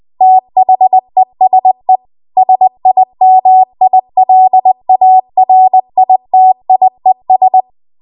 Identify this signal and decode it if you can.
Morse code, heard as a keyed tone, about 20 wpm, 750 Hz. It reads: THESE SIMILARITIES
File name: Archive / 20 WPM / 20WPM